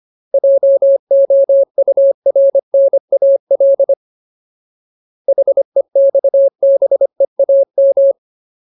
Morse code, heard as a keyed tone, about 25 wpm, 550 Hz. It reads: JOURNAL HEXBEAM
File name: Ultimate Word List / 25WPM Ultimate